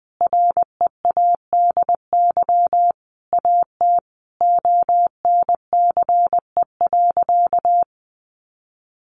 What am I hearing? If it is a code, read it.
Morse code, heard as a keyed tone, about 19 wpm, 700 Hz. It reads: READY AT ONCE.
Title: Sentences from Top 400 Words 20wpm